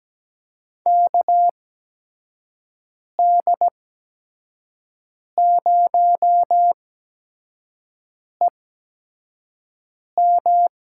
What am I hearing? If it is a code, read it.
Morse code, heard as a keyed tone, about 17 wpm, spaced as for 5 wpm, 700 Hz. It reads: KD0EM
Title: USA Call Signs - 6 Characters - 8x Character Spacing - 17wpm